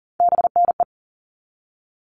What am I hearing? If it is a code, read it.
Morse code, heard as a keyed tone, about 40 wpm, 700 Hz. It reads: BNE
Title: All Cut Numbers - 3-Digit 40wpm